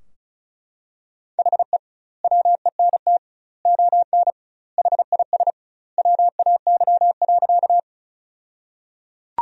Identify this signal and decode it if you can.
Morse code, heard as a keyed tone, about 34 wpm, 700 Hz. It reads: HE WENT ON HIS WAY.
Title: 2x Word-Spacing - Sentences from Top 200 Words - 35wpm